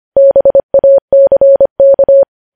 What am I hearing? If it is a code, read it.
Morse code, heard as a keyed tone, about 25 wpm, 550 Hz. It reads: BACK
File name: podcast / News headlines-1x-25x25